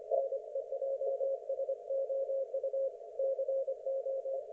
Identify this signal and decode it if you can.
Morse code, heard as a keyed tone, about 26 wpm, 540 Hz. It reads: NEJSOU CO
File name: Archiv / CUC179 / ok1fhi cul179